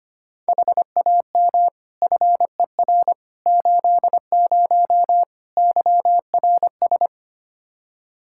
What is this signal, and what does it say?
Morse code, heard as a keyed tone, about 25 wpm, 700 Hz. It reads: HAM FER 80 YRS
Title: Practice QSO - Exchange 3 - More Info - 25wpm